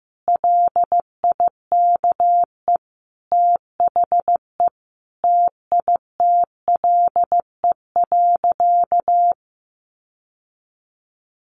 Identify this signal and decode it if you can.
Morse code, heard as a keyed tone, about 15 wpm, 700 Hz. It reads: LIKE THE TITLE.
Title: Sentences from Top 1200 Words 15wpm